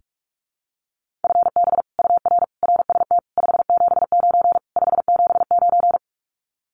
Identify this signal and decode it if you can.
Morse code, heard as a keyed tone, about 44 wpm, 700 Hz. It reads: FB UR RST 579 579
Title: Common QSO Phrases Encore 45wpm